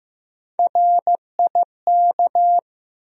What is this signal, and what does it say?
Morse code, heard as a keyed tone, about 15 wpm, 700 Hz. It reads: RIK
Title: USA Call Sign Postfixes - 3 Letters 15wpm